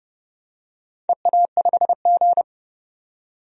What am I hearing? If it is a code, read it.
Morse code, heard as a keyed tone, about 32 wpm, 700 Hz. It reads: EA5G
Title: DXCC-100 Call Signs - 4 Characters - 30wpm